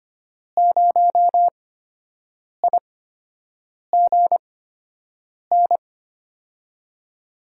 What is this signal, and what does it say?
Morse code, heard as a keyed tone, about 25 wpm, spaced as for 7 wpm, 700 Hz. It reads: 0IGN
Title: USA Call Signs - 6 Characters - 8x Character Spacing - 25wpm